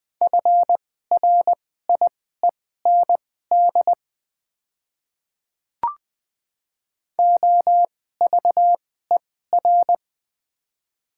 Morse code, heard as a keyed tone, about 20 wpm, spaced as for 14 wpm, 700 Hz. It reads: FRIEND OVER
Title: 2x Character Spacing - Top 300 Words - 20wpm